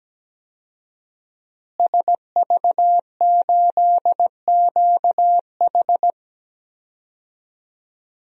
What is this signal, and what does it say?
Morse code, heard as a keyed tone, about 17 wpm, 700 Hz. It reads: SV8QH
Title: DXCC-100 Call Signs - 5 Characters - Repeated 2x - 17wpm